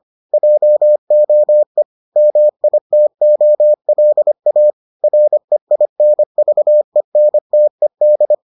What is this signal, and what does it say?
Morse code, heard as a keyed tone, about 25 wpm, 600 Hz. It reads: JOE MITOLA REINVENTED